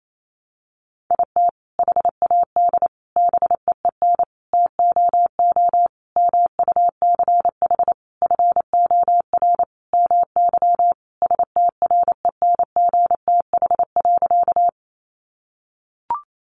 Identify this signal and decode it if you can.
Morse code, heard as a keyed tone, about 28 wpm, 700 Hz. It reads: IT HAD BEEN TOO MUCH FOR MY STRENGTH.
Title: Sentences from Top 1300 Words 28wpm